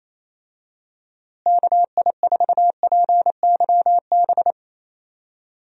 Morse code, heard as a keyed tone, about 28 wpm, 700 Hz. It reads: KI4PYB
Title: USA Call Signs - 6 Characters - Repeated 3x - 28wpm